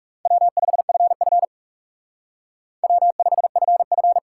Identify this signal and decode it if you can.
Morse code, heard as a keyed tone, about 38 wpm, 700 Hz. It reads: W5FF W5FF